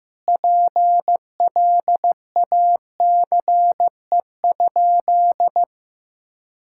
Morse code, heard as a keyed tone, about 15 wpm, 700 Hz. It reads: PLACE?
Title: Sentences from Top 1600 Words 15wpm